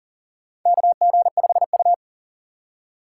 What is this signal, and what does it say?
Morse code, heard as a keyed tone, about 40 wpm, 700 Hz. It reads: KG5U
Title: USA Call Signs - 3x Character Spacing - 40wpm